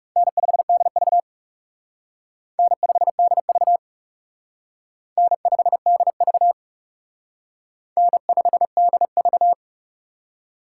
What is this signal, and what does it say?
Morse code, heard as a keyed tone, about 37 wpm, 700 Hz. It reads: N5DV N5DV N5DV N5DV
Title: Speed Racing - USA 1x2 Call Signs 30wpm